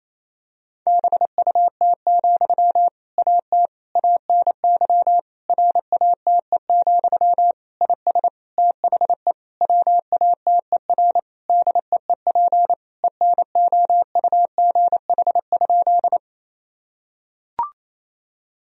Morse code, heard as a keyed tone, about 28 wpm, 700 Hz. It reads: BUT, AT ANY RATE, IS THE WATER DEEP ENOUGH?
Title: Sentences from Top 1000 Words - Review Part 4 - 28wpm